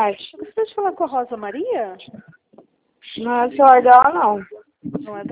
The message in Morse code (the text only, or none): none